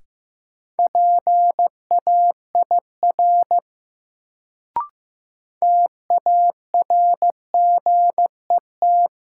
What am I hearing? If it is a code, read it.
Morse code, heard as a keyed tone, about 15 wpm, 700 Hz. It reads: PAIR TARGET
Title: Sentences from Top 1200 Words - New Word Drill - 15wpm